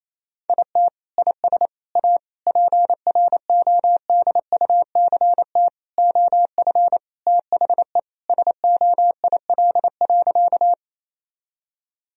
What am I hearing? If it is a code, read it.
Morse code, heard as a keyed tone, about 28 wpm, 700 Hz. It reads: IT IS A PRODUCT OF THE SOIL.
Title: Sentences from Top 1000 Words - Review Part 4 - 28wpm